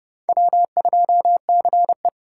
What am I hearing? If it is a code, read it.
Morse code, heard as a keyed tone, about 30 wpm, 700 Hz. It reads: W2CE
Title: USA Call Signs - 4 Characters - 2x Character Spacing - 30wpm